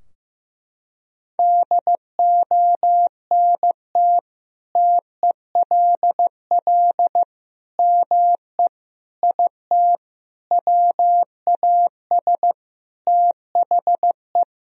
Morse code, heard as a keyed tone, about 15 wpm, 700 Hz. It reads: DONT TELL ME IT WAS THE
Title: Sentences from Top 1000 Words - Review Part 3 - 15wpm